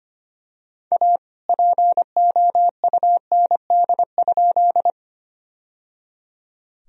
Morse code, heard as a keyed tone, about 25 wpm, 700 Hz. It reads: A POUND?
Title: Sentences from Top 1000 Words - Review Part 2 - 25wpm